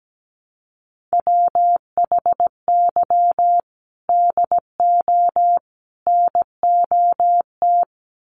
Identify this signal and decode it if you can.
Morse code, heard as a keyed tone, about 17 wpm, 700 Hz. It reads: WHY DO NOT
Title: Sets of 3 Words 17wpm